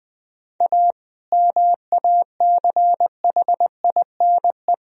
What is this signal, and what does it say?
Morse code, heard as a keyed tone, about 20 wpm, 700 Hz. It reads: A MACHINE
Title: Sentences from Top 1000 Words - Review Part 2 - 20wpm